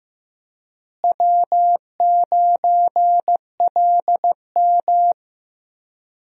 Morse code, heard as a keyed tone, about 15 wpm, 700 Hz. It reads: W9LM
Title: USA Call Signs - 4 Characters - Repeated 2x - 15wpm